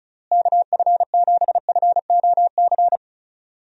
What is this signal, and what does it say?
Morse code, heard as a keyed tone, about 36 wpm, 700 Hz. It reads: KF7FOC
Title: USA Call Signs - 6 Characters - Repeated 2x - 35wpm